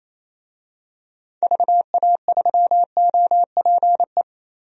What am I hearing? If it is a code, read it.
Morse code, heard as a keyed tone, about 28 wpm, 700 Hz. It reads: VA3OPE
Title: DXCC-100 Call Signs - 6 Characters - Repeated 2x - 28wpm